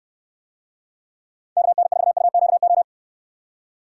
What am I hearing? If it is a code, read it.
Morse code, heard as a keyed tone, about 63 wpm, 700 Hz. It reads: KN4UQO